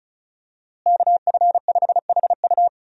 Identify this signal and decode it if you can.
Morse code, heard as a keyed tone, about 36 wpm, 700 Hz. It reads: KF5HU
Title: USA Call Signs - 5 Characters - Repeated 2x - 35wpm